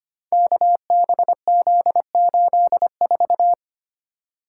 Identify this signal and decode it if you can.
Morse code, heard as a keyed tone, about 25 wpm, 700 Hz. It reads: KBZ84
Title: Code Groups - 5 Letter:Numbers - 25wpm